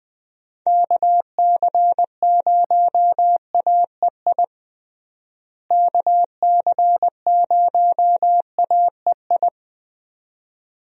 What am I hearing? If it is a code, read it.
Morse code, heard as a keyed tone, about 20 wpm, 700 Hz. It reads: KC0AEI KC0AEI
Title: USA Call Signs - 6 Characters - Repeated 2x - 20wpm